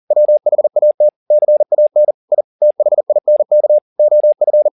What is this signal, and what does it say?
Morse code, heard as a keyed tone, about 40 wpm, 600 Hz. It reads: WHAT CAN I THINK OF
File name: the old man and the sea 2 40wpm